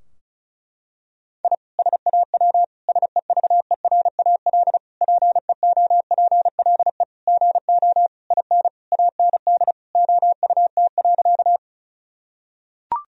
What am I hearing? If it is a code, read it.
Morse code, heard as a keyed tone, about 34 wpm, 700 Hz. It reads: I SAW SEVERAL PEOPLE GO IN AND OUT.
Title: Sentences from Top 1000 Words - Review Part 2 - 35wpm